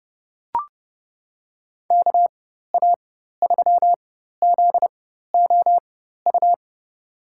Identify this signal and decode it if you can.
Morse code, heard as a keyed tone, about 32 wpm, spaced as for 14 wpm, 700 Hz. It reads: KA3ZOU
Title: USA Call Signs - 6 Characters - 4x Character Spacing - 30wpm